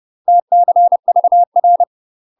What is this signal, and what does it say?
Morse code, heard as a keyed tone, about 30 wpm, 700 Hz. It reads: TCVR
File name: UK MASTER QSOs TRAINING / 30WPM / UK MASTER QSOs 0021